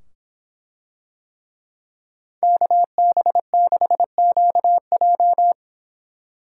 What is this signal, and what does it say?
Morse code, heard as a keyed tone, about 25 wpm, 700 Hz. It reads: KB6QJ